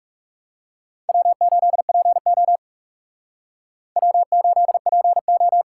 Morse code, heard as a keyed tone, about 35 wpm, 700 Hz. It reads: W8PO W8PO
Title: Speed Racing - USA 1x2 Call Signs 30wpm